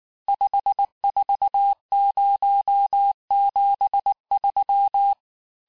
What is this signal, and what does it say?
Morse code, heard as a keyed tone, about 19 wpm, 800 Hz. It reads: 54073
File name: cours-du-lundi files / cours-1583--texte-chiffre-18-mpm